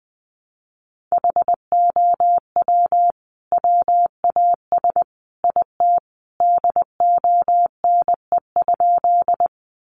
Morse code, heard as a keyed tone, about 20 wpm, 700 Hz. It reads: HOW WAS IT DONE?
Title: Sentences from Top 500 Words 20wpm